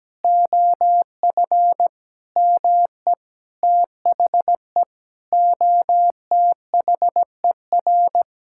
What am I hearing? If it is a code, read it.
Morse code, heard as a keyed tone, about 17 wpm, 700 Hz. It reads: OF ME THE OTHER
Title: Sentences from Top 1000 Words - Review Part 5 - 17wpm